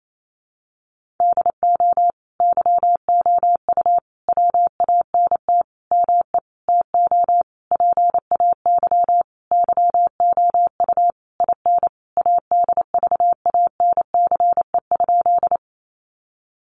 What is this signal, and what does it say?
Morse code, heard as a keyed tone, about 28 wpm, 700 Hz. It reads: DO YOU WANT ME TO PAY YOU IN ADVANCE?
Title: Sentences from Top 1400 Words 28wpm